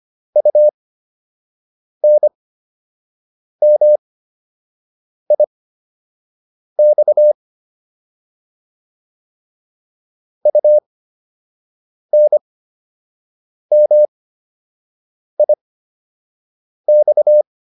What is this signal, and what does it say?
Morse code, heard as a keyed tone, about 25 wpm, spaced as for 6 wpm, 600 Hz. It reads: UNMIX UNMIX